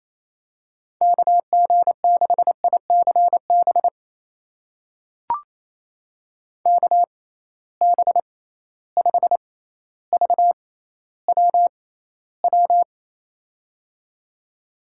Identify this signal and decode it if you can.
Morse code, heard as a keyed tone, about 27 wpm, spaced as for 14 wpm, 700 Hz. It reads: KG6ICB KB5VWW
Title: USA Call Signs - 6x Character Spacing - 28wpm